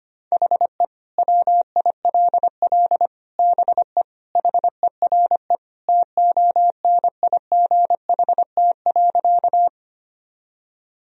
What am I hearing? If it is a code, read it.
Morse code, heard as a keyed tone, about 25 wpm, 700 Hz. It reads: HE WILL BE HERE TONIGHT.
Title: Sentences from Top 1000 Words - Review Part 3 - 25wpm